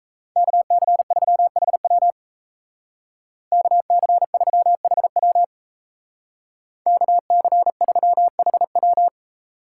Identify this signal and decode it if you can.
Morse code, heard as a keyed tone, about 33 wpm, 700 Hz. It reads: KC3HW KC3HW KC3HW
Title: Speed Racing - USA 2x2 Call Signs 28wpm